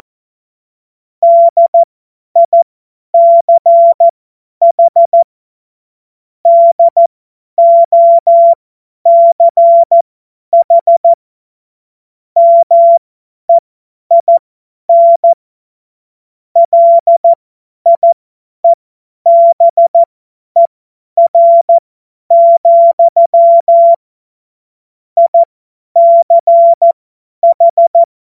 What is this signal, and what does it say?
Morse code, heard as a keyed tone, about 14 wpm, spaced as for 10 wpm, 680 Hz. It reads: DICH DOCH MEIN LIEBER, ICH
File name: Morsekurs / Witze / Witz 08. Februar 2022 50 F70